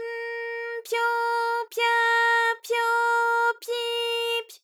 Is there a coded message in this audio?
No.